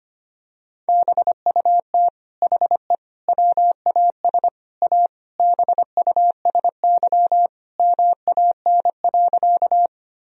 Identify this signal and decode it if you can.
Morse code, heard as a keyed tone, about 25 wpm, 700 Hz. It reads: BUT HE WAS A BUSY MAN.
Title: Sentences from Top 1000 Words - Review Part 3 - 25wpm